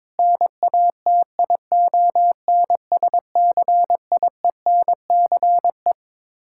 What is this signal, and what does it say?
Morse code, heard as a keyed tone, about 22 wpm, 700 Hz. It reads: NATION SCIENCE
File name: practice / 22wpm / Build A Phrase - 5-Word Phrases - 22wpm